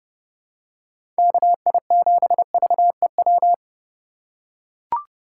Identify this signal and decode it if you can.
Morse code, heard as a keyed tone, about 30 wpm, 700 Hz. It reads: KI7VEW
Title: USA Call Signs - 3x Character Spacing - 30wpm